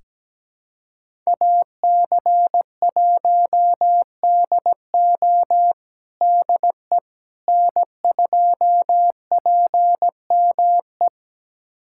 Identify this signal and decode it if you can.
Morse code, heard as a keyed tone, about 17 wpm, 700 Hz. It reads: AC1DO DE N2PME
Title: USA Callsign DE Callsign - 5 Characters - 17wpm